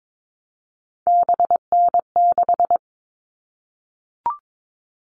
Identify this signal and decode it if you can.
Morse code, heard as a keyed tone, about 22 wpm, 700 Hz. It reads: BN6
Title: All Cut Numbers - 3-Digit 22wpm